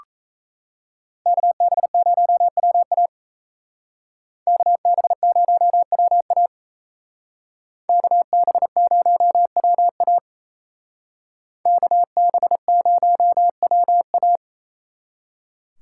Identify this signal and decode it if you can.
Morse code, heard as a keyed tone, about 33 wpm, 700 Hz. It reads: KB0WA KB0WA KB0WA KB0WA
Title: Speed Racing - USA 2x2 Call Signs 28wpm